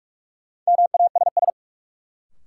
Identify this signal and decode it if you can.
Morse code, heard as a keyed tone, about 46 wpm, 700 Hz. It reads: MASS
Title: Sets of 5 Words Encore - New Word Drill - 45wpm